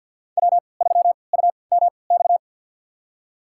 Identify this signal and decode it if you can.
Morse code, heard as a keyed tone, about 53 wpm, spaced as for 29 wpm, 700 Hz. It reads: W3UMX
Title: USA Call Signs - 5 Characters - 3x Character Spacing - 50wpm